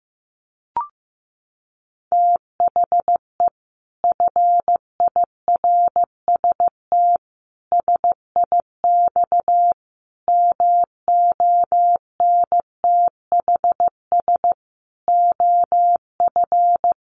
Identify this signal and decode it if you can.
Morse code, heard as a keyed tone, about 15 wpm, 700 Hz. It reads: THE FIRST SIX MONTHS OF
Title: Sets of 5 Words Encore2 15wpm